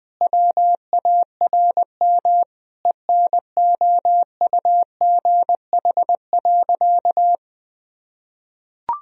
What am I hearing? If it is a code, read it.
Morse code, heard as a keyed tone, about 20 wpm, 700 Hz. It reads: WARM ENOUGH.
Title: Sentences from Top 1000 Words - Review Part 2 - 20wpm